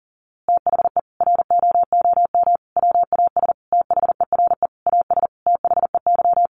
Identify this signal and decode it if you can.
Morse code, heard as a keyed tone, about 40 wpm, 700 Hz. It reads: THE ROOM WAS THERE AS THEY
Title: A Sound Of Thunder 40wpm